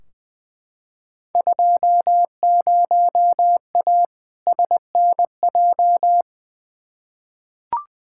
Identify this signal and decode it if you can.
Morse code, heard as a keyed tone, about 20 wpm, 700 Hz. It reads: 20A SNJ
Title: Field Day Exchanges Encore - 20wpm